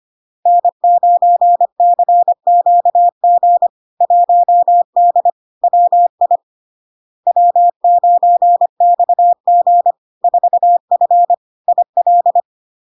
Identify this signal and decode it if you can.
Morse code, heard as a keyed tone, about 25 wpm, 700 Hz. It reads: N9CQG 1D WI W9XG 4F IL
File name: LARK-FD-2022-Log-in-CW-25wpm-700-Hz